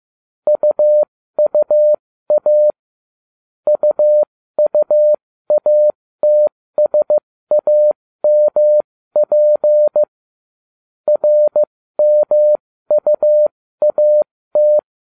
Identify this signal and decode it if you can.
Morse code, heard as a keyed tone, about 15 wpm, spaced as for 13 wpm, 600 Hz. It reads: UUA UUATSAMP RMUAT